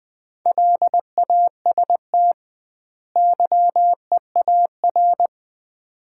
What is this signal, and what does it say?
Morse code, heard as a keyed tone, about 20 wpm, 700 Hz. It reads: LAST YEAR
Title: 2x Word-Spacing - Sets of 2 Words - 20wpm